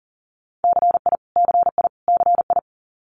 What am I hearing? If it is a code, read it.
Morse code, heard as a keyed tone, about 41 wpm, 700 Hz. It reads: CI CI CI
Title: Multi-letter Phonemes Repeated 3x 40wpm